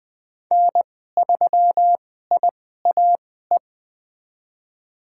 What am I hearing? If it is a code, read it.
Morse code, heard as a keyed tone, about 20 wpm, spaced as for 14 wpm, 700 Hz. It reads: N3IAE
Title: USA Call Signs - 2x Character Spacing - 20wpm